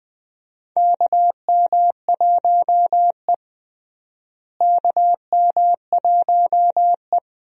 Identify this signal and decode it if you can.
Morse code, heard as a keyed tone, about 20 wpm, 700 Hz. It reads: KM1E KM1E
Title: USA Call Signs - 4 Characters - Repeated 2x - 20wpm